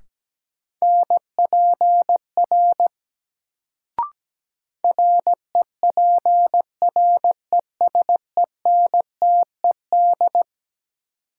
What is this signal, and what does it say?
Morse code, heard as a keyed tone, about 17 wpm, 700 Hz. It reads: NPR REPRESENTED
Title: Sets of 5 Words - New Word Drill - 17wpm